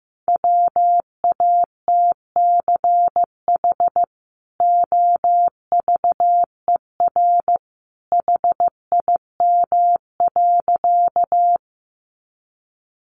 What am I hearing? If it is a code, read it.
Morse code, heard as a keyed tone, about 15 wpm, 700 Hz. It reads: WATCH OVER HIM.
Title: Sentences from Top 300 Words 15wpm